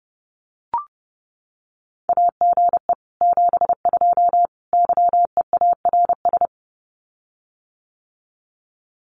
Morse code, heard as a keyed tone, about 30 wpm, 700 Hz. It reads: AGE 72 YEARS
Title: Common QSO Phrases Encore 30wpm